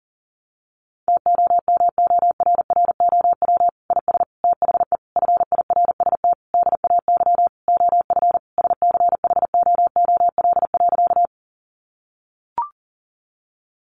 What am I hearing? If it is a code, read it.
Morse code, heard as a keyed tone, about 40 wpm, 700 Hz. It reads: TOMORROW IS THE FIRST DAY OF SCHOOL.
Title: Additional Easy Phrases 40wpm